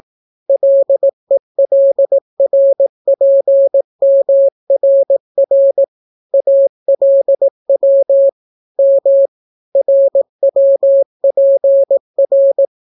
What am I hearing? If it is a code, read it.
Morse code, heard as a keyed tone, about 18 wpm, 550 Hz. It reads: LELRPMRR ALW M RWPR